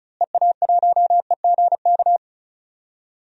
Morse code, heard as a keyed tone, about 36 wpm, 700 Hz. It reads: EA1EGK
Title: DXCC-100 Call Signs - 6 Characters - Repeated 2x - 35wpm